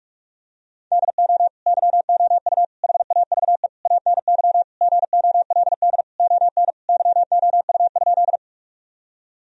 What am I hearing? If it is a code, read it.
Morse code, heard as a keyed tone, about 43 wpm, 700 Hz. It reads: DO YOU HAVE ANY GOLD ON YOU?
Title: Sentences from Top 1000 Words - Review Part 3 - 45wpm